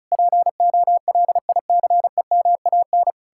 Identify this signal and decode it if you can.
Morse code, heard as a keyed tone, about 48 wpm, spaced as for 42 wpm, 700 Hz. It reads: POLICEMAN
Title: Speed-Racing Common QSO Phrases - 35wpm